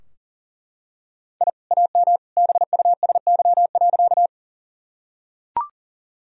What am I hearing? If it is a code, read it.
Morse code, heard as a keyed tone, about 37 wpm, 700 Hz. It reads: I AM BUSY.
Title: Sentences from Top 1000 Words - Review Part 3 - 40wpm